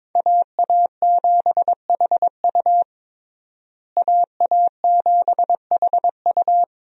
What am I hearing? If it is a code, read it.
Morse code, heard as a keyed tone, about 22 wpm, 700 Hz. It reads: AA7HU AA7HU
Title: USA Call Signs - 5 Characters - Repeated 2x - 22wpm